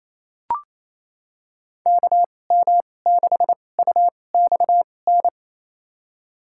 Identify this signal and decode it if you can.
Morse code, heard as a keyed tone, about 27 wpm, spaced as for 20 wpm, 700 Hz. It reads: KM6UXN
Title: USA Call Signs - 2x Character Spacing - 28wpm